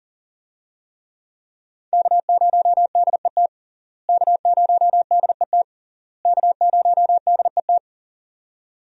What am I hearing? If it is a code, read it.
Morse code, heard as a keyed tone, about 40 wpm, 700 Hz. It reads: K0DET K0DET K0DET
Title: USA Call Signs - 5 Characters - Repeated 3x - 40wpm